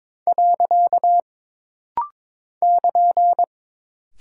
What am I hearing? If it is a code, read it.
Morse code, heard as a keyed tone, about 22 wpm, 700 Hz. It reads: . <KN>
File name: practice / 22wpm / Single Punctuation-Prosign - Rapid-Fire - 22wpm